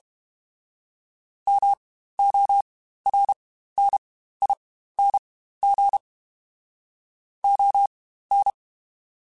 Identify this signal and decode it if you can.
Morse code, heard as a keyed tone, about 32 wpm, spaced as for 15 wpm, 800 Hz. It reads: MORNING ON
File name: Spaced Characters / Spaced Characters-mp3s / Weather 4 24 sp @30